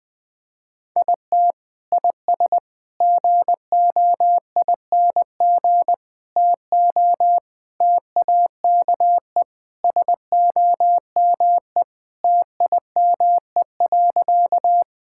Morse code, heard as a keyed tone, about 20 wpm, 700 Hz. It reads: IT IS GOING TO TAKE SOME TIME.
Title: Sentences with 3-Word Phrases - 20wpm